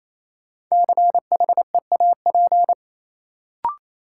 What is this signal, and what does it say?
Morse code, heard as a keyed tone, about 28 wpm, 700 Hz. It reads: CHEAP
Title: Sentences from Top 1200 Words - New Word Drill Alt - 28wpm